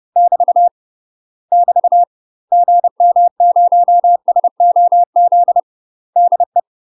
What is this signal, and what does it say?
Morse code, heard as a keyed tone, about 30 wpm, 700 Hz. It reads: <BT> <BT> GM0SOZ DE
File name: UK MASTER QSOs TRAINING / 30WPM / UK MASTER QSOs 0033